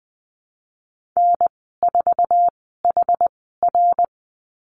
Morse code, heard as a keyed tone, about 20 wpm, spaced as for 14 wpm, 700 Hz. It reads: N4HR
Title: 062 Lesson 11g - Letter h - Call Signs 20wpm